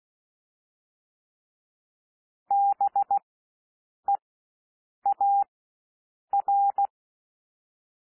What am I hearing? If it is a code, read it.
Morse code, heard as a keyed tone, about 16 wpm, spaced as for 7 wpm, 800 Hz. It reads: BEAR
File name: SOUND WORDS 16 wpm